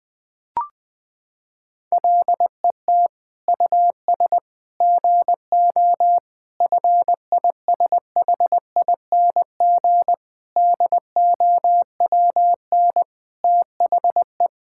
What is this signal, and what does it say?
Morse code, heard as a keyed tone, about 20 wpm, 700 Hz. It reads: LET US GO FISHING DOWN THE
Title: Sentences from Top 1000 Words - Review Part 5 - 20wpm